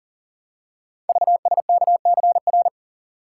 Morse code, heard as a keyed tone, about 42 wpm, 700 Hz. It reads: VSKCR
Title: Code Groups - 5 Letters - 40wpm